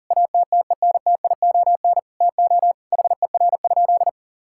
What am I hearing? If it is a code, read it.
Morse code, heard as a keyed tone, about 40 wpm, 700 Hz. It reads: ATTENTION TO HER?
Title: Sentences from Top 1000 Words - Review Part 6 - 40wpm